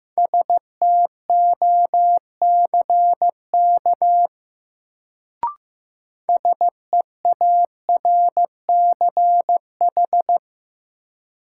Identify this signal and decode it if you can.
Morse code, heard as a keyed tone, about 15 wpm, 700 Hz. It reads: STOCK SEARCH
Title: Sentences from Top 800 Words - New Word Drill - 15wpm